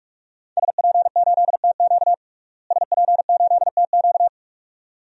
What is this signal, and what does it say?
Morse code, heard as a keyed tone, about 46 wpm, 700 Hz. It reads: SP8TQ SP8TQ
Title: DXCC-100 Call Signs - 5 Characters - Repeated 2x - 45wpm